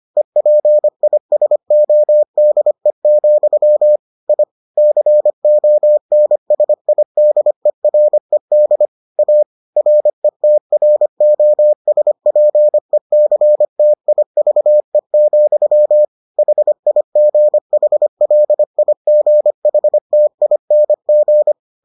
Morse code, heard as a keyed tone, about 25 wpm, 600 Hz. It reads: EPISODE, I CONSIDERED A RETROSPECTIVE, HIGHLIGHTING